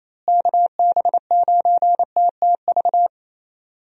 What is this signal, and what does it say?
Morse code, heard as a keyed tone, about 28 wpm, 700 Hz. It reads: KB9TTV
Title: USA Call Signs - 6 Characters - Repeated 3x - 28wpm